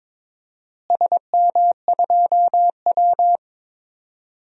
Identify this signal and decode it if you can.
Morse code, heard as a keyed tone, about 22 wpm, 700 Hz. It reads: SM2W
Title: DXCC-100 Call Signs - 4 Characters - Repeated 3x - 22wpm